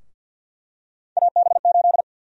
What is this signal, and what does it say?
Morse code, heard as a keyed tone, about 50 wpm, 700 Hz. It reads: AB8
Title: USA Call Sign Prefixes - 50wpm